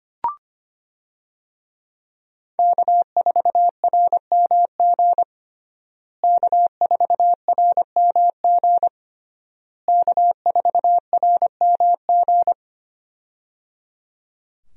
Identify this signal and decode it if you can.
Morse code, heard as a keyed tone, about 25 wpm, 700 Hz. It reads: K4RMG K4RMG K4RMG
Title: USA Call Signs - 5 Characters - Repeated 3x - 25wpm